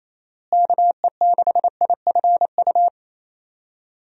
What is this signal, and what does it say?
Morse code, heard as a keyed tone, about 28 wpm, 700 Hz. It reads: KE6IFU
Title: USA Call Signs - 6 Characters - Repeated 2x - 28wpm